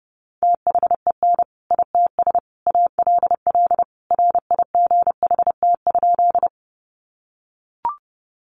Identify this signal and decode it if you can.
Morse code, heard as a keyed tone, about 29 wpm, 700 Hz. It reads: THEN ITS ALL RIGHT?
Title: Sentences from Top 600 Words 30wpm